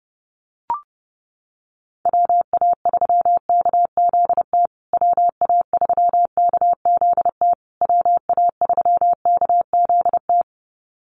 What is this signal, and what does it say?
Morse code, heard as a keyed tone, about 30 wpm, 700 Hz. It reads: WA3KZT WA3KZT WA3KZT
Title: USA Call Signs Repeated 3x 30wpm